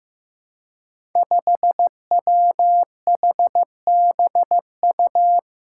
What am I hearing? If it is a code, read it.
Morse code, heard as a keyed tone, about 15 wpm, 700 Hz. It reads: 5WHBU
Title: Five Challenging Letter-Number Combinations 15wpm